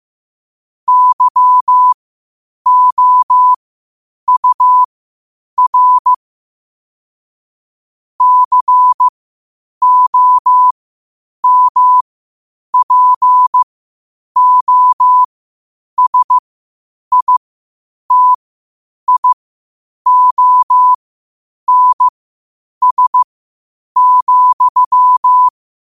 Morse code, heard as a keyed tone, about 15 wpm, spaced as for 8 wpm, 1000 Hz. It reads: YOUR COMPOSITIONS,
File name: qotdfiles / 8wpm